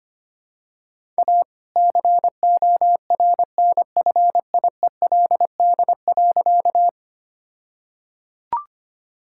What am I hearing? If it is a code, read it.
Morse code, heard as a keyed tone, about 25 wpm, 700 Hz. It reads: A CORNFIELD.
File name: practice / 25wpm / All-Age Appropriate Jokes - 25wpm